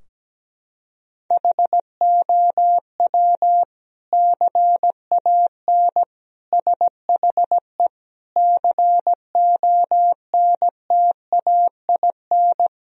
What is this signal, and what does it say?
Morse code, heard as a keyed tone, about 17 wpm, 700 Hz. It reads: HOW CAN SHE CONTAIN
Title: Sentences from Top 1000 Words - Review Part 2 - 17wpm